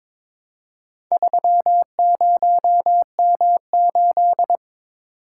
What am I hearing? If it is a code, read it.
Morse code, heard as a keyed tone, about 22 wpm, 700 Hz. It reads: 30M8
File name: practice / 22wpm / Four Letter-Number Combinations - 22wpm